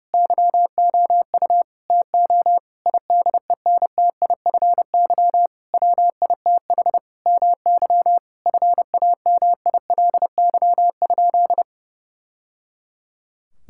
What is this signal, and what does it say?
Morse code, heard as a keyed tone, about 32 wpm, 700 Hz. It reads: YOU TO IDENTIFY WITH MY FAMILY?
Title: Sentences from Top 1000 Words - Review Part 6 - 30wpm